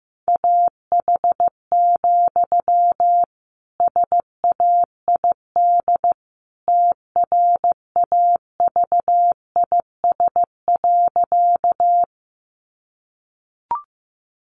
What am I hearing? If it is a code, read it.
Morse code, heard as a keyed tone, about 15 wpm, 700 Hz. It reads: AH, SAID TRAVIS.
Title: A Sound Of Thunder 15wpm